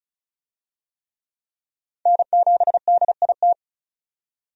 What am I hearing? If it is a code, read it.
Morse code, heard as a keyed tone, about 34 wpm, 700 Hz. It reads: N7DIT
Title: USA Call Signs - 5 Characters - 35wpm